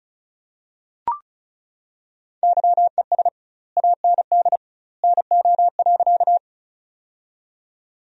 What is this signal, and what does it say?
Morse code, heard as a keyed tone, about 34 wpm, 700 Hz. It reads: YES AND NO.
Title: 2x Word-Spacing - Sentences from Top 500 Words - 35wpm